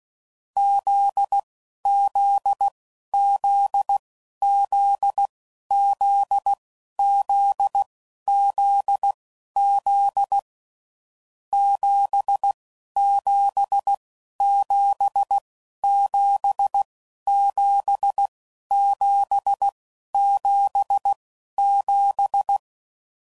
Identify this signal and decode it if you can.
Morse code, heard as a keyed tone, about 16 wpm, spaced as for 11 wpm, 800 Hz. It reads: ZZZZZZZZ 77777777
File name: Mixed / Short Runs / GZ GZ7